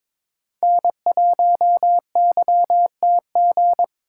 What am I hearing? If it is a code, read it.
Morse code, heard as a keyed tone, about 22 wpm, 700 Hz. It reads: N1YTG
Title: USA Call Signs - 5 Characters - 3x Character Spacing - 22wpm